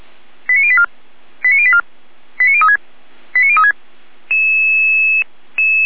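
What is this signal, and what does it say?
Five-tone selective call (ZVEI), heard as quick runs of five tones, tones of about 70 ms. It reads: 89084 89084 89036 89036
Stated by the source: ZVEI five-tone selective call received by SDR, from feuerwehr-orlamuende.org (Feuerwehr Orlamünde fire brigade) server.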